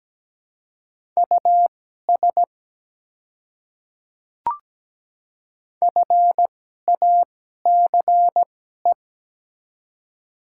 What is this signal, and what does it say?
Morse code, heard as a keyed tone, about 17 wpm, spaced as for 12 wpm, 700 Hz. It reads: US FACE
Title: 2x Character Spacing - Top 300 Words - 17wpm